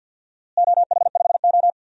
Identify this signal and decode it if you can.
Morse code, heard as a keyed tone, about 52 wpm, 700 Hz. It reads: OH5O
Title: DXCC-100 Call Signs - 4 Characters - Repeated 3x - 50wpm